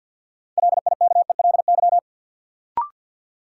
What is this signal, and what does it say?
Morse code, heard as a keyed tone, about 51 wpm, 700 Hz. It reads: LIKELY
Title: Sentences from Top 600 Words - New Word Drill - 50wpm